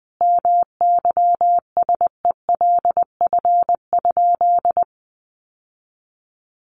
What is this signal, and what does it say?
Morse code, heard as a keyed tone, about 20 wpm, 700 Hz. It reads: MYSELF?
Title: Sentences from Top 1300 Words 20wpm